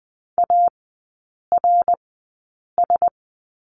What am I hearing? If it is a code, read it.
Morse code, heard as a keyed tone, about 20 wpm, spaced as for 8 wpm, 700 Hz. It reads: ARS
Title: 067 Lesson 12e - Letter d - Three Characters 20wpm